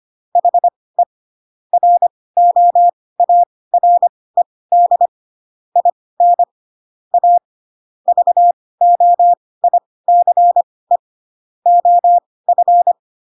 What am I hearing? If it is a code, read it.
Morse code, heard as a keyed tone, about 25 wpm, spaced as for 18 wpm, 700 Hz. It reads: HE ROARED IN A VOICE OF
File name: Pinocchio ch3 12 25 ik1hge